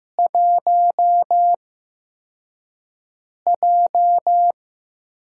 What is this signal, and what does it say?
Morse code, heard as a keyed tone, about 15 wpm, spaced as for 4 wpm, 700 Hz. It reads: 1J
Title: USA Call Signs - 8x Character Spacing - 15wpm